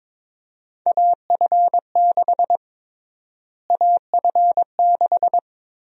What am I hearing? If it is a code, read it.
Morse code, heard as a keyed tone, about 23 wpm, 700 Hz. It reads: AF6 AF6
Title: USA Call Sign Prefixes - Repeated 2x - 22wpm